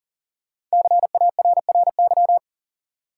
Morse code, heard as a keyed tone, about 37 wpm, 700 Hz. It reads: CARRY
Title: Sentences from Top 400 Words - New Word Drill - 40wpm